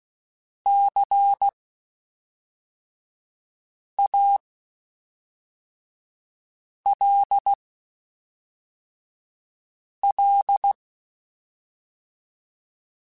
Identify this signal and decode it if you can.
Morse code, heard as a keyed tone, about 16 wpm, spaced as for 3 wpm, 800 Hz. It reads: CALL
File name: (spaced) / QSOs Spaced-mp3s / kc7haa @5 sp